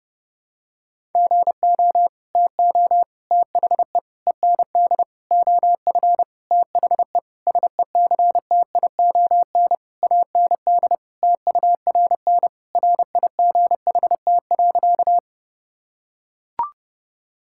Morse code, heard as a keyed tone, about 30 wpm, 700 Hz. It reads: GO TO THE END OF THE SECTION AND TURN RIGHT.
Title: Sentences from Top 1000 Words - Review Part 5 - 30wpm